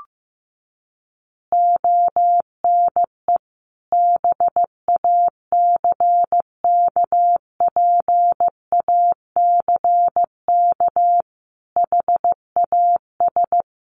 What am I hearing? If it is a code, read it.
Morse code, heard as a keyed tone, about 15 wpm, 700 Hz. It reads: ONE BACKPACK HAS
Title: Additional Easy Phrases 15wpm